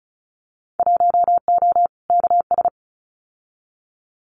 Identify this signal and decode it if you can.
Morse code, heard as a keyed tone, about 36 wpm, 700 Hz. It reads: 1O KS
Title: Winter Field Day Exchanges 35wpm